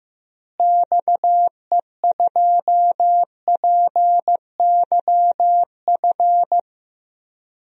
Morse code, heard as a keyed tone, about 15 wpm, 700 Hz. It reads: XE2PYF
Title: DXCC-100 Call Signs - 6 Characters - Repeated 2x - 15wpm